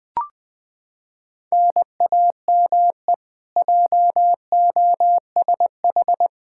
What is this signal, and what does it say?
Morse code, heard as a keyed tone, about 20 wpm, 700 Hz. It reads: NAME JOSH
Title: K1USN SST QSO Practice 20wpm